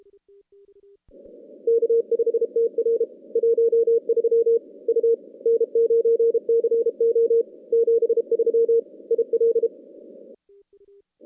Morse code, heard as a keyed tone, about 32 wpm, 460 Hz. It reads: K5TR 13 U N9CO 73 IL
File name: contests / CW-SS-k5tr- -2100